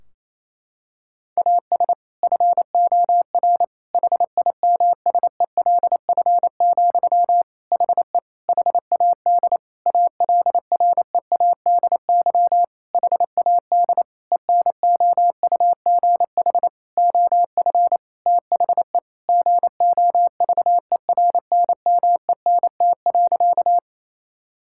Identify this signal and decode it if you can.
Morse code, heard as a keyed tone, about 28 wpm, 700 Hz. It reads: AS FOR HIMSELF, HE HAD ALREADY HAD ENOUGH OF THE GOVERNMENT.
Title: Sentences from Top 1600 Words 28wpm